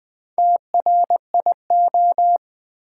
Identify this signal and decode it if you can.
Morse code, heard as a keyed tone, about 20 wpm, 700 Hz. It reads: TRIO
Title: Build A Word - 2 Letters At A Time - 4-Letter Words - 20wpm